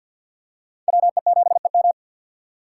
Morse code, heard as a keyed tone, about 49 wpm, 700 Hz. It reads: WE7EM
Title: USA Call Signs - 5 Characters - 6x Character Spacing - 50wpm